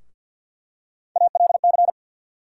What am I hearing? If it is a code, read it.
Morse code, heard as a keyed tone, about 48 wpm, 700 Hz. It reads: ALC
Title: USA Call Sign Postfixes - 3 Letters 50wpm